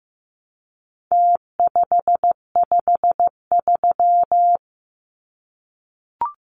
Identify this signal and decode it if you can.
Morse code, heard as a keyed tone, about 15 wpm, 700 Hz. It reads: T553
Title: Numbers - Contest Style - 4-Digit 15wpm